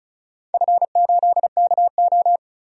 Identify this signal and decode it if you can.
Morse code, heard as a keyed tone, about 35 wpm, 700 Hz. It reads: F8KO
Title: DXCC-100 Call Signs - 4 Characters - 35wpm